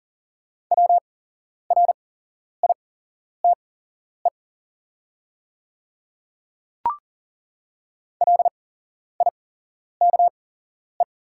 Morse code, heard as a keyed tone, about 41 wpm, spaced as for 11 wpm, 690 Hz. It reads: WRITE LIKE
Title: 8x Character Spacing - Top 100 Words - 40wpm